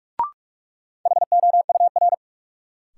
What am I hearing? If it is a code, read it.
Morse code, heard as a keyed tone, about 48 wpm, 700 Hz. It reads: HOUR
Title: Sets of 5 Words - New Word Drill - 45wpm